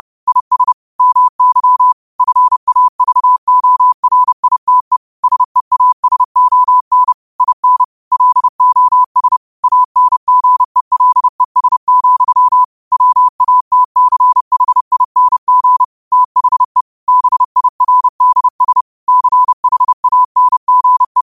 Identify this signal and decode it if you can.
Morse code, heard as a keyed tone, about 30 wpm, 1000 Hz. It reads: IS MY FAVORITE SEASON IN LOS ANGELES, WATCHING THE BIRDS CHANGE
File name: qotdfiles / 30wpm